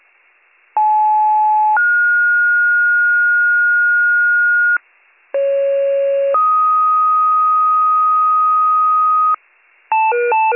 Two-tone page, heard as a one-second tone, then a three-second tone, about 860 then 1430 Hz, about 550 then 1250 Hz, with noise. It is heard from 0.8 to 9.4 s.